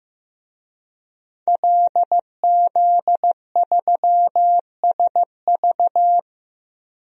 Morse code, heard as a keyed tone, about 15 wpm, 700 Hz. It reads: LZ3SV
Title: DXCC-100 Call Signs - 5 Characters - Repeated 2x - 15wpm